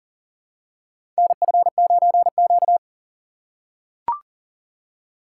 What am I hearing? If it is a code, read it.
Morse code, heard as a keyed tone, about 40 wpm, 700 Hz. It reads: NF9Q
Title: USA Call Signs - 8x Character Spacing - 40wpm